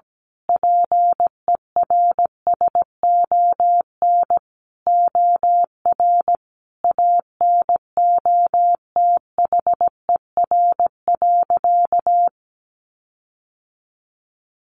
Morse code, heard as a keyed tone, about 17 wpm, 700 Hz. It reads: PERSON OR ANOTHER.
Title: Sentences from Top 1400 Words 17wpm